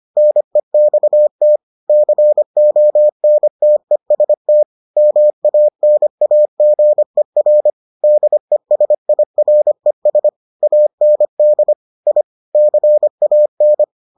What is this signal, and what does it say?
Morse code, heard as a keyed tone, about 25 wpm, 600 Hz. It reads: NEXT CONTEST MANAGER DESIRES AND I CAN